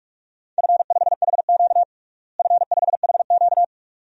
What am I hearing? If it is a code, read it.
Morse code, heard as a keyed tone, about 42 wpm, 700 Hz. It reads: F5HQ F5HQ
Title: DXCC-100 Call Signs - 4 Characters - Repeated 2x - 45wpm